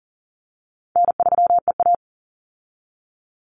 Morse code, heard as a keyed tone, about 40 wpm, 700 Hz. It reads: N3EA
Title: Easy Call Signs - 40wpm